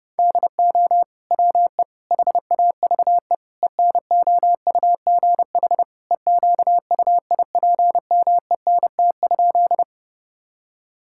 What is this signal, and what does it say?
Morse code, heard as a keyed tone, about 30 wpm, 700 Hz. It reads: DO WE HAVE ENOUGH EQUIPMENT?
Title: Sentences from Top 1000 Words - Review Part 5 - 30wpm